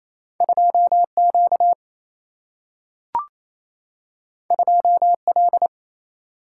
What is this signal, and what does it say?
Morse code, heard as a keyed tone, about 28 wpm, 700 Hz. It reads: 2 Q 2 L
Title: Single-Letter Single-Number 28wpm